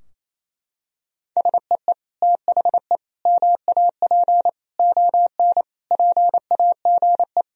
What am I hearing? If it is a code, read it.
Morse code, heard as a keyed tone, about 28 wpm, 700 Hz. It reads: SEE THE MAP ON PAGE
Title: Sentences from Top 1000 Words - Review Part 2 - 28wpm